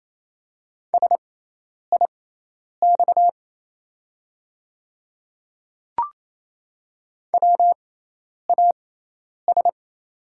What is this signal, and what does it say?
Morse code, heard as a keyed tone, about 28 wpm, spaced as for 10 wpm, 700 Hz. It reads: SIX WAS
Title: 6x Character Spacing - Top 500 Words - 28wpm